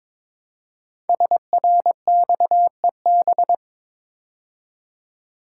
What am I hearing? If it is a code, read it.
Morse code, heard as a keyed tone, about 22 wpm, 700 Hz. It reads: SRXEB
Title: Code Groups - 5 Letters - 22wpm